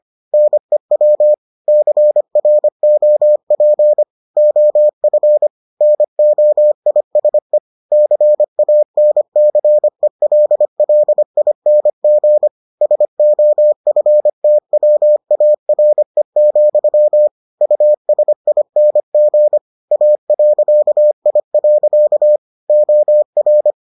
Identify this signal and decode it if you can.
Morse code, heard as a keyed tone, about 25 wpm, 600 Hz. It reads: NEW CROP OF NOISE CANCELLING SOFTWARE, USING A.I. OR